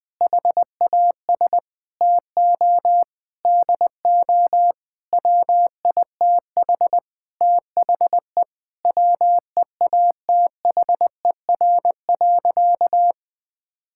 Morse code, heard as a keyed tone, about 21 wpm, 700 Hz. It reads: HAS TO DO WITH THE WEATHER.
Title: Sentences with 9-Word Phrases - 20wpm